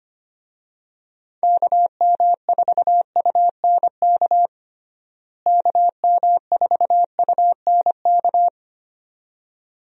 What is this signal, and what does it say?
Morse code, heard as a keyed tone, about 25 wpm, 700 Hz. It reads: KM4UNK KM4UNK
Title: USA Call Signs - 6 Characters - Repeated 2x - 25wpm